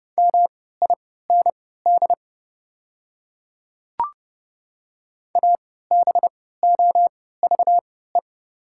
Morse code, heard as a keyed tone, about 30 wpm, spaced as for 17 wpm, 700 Hz. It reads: MIND ABOVE
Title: 3x Character Spacing - Top 500 Words - 30wpm